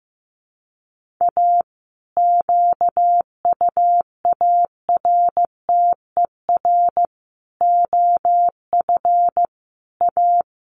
Morse code, heard as a keyed tone, about 15 wpm, 700 Hz. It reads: A QUARTER OF A
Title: Sets of 5 Words Encore2 15wpm